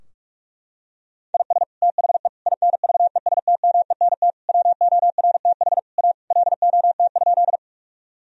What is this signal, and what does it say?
Morse code, heard as a keyed tone, about 46 wpm, 700 Hz. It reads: IS THE INVESTMENT WORTH A LOT?
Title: Sentences from Top 1000 Words - Review Part 4 - 45wpm